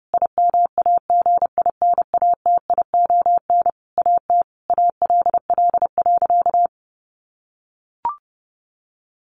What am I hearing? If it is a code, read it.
Morse code, heard as a keyed tone, about 30 wpm, 700 Hz. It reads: IMAGINATION AT ALL.
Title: Sentences from Top 1300 Words 30wpm